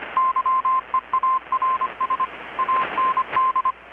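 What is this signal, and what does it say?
Morse code, heard as a keyed tone, about 25 wpm, 1040 Hz, with radio noise. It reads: YEARS AND
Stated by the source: Morse from KPH's Night of Nights XX maritime radiotelegraphy event, received in Chiapas, Mexico about 3,800 km away on a Kenwood R-600 with a grounded 10 m long-wire antenna.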